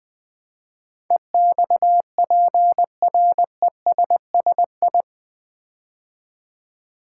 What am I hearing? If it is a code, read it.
Morse code, heard as a keyed tone, about 20 wpm, 700 Hz. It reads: EXPRESSI
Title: Build A Word - 4 Letters At A Time - 10-Letter Words - 20wpm